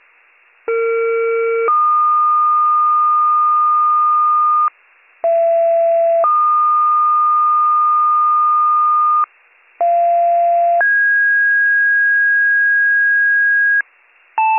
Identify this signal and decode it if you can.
Two-tone page, heard as a one-second tone, then a three-second tone, about 460 then 1220 Hz, about 670 then 1220 Hz, about 690 then 1650 Hz. It is heard from 0.7 to 13.8 s.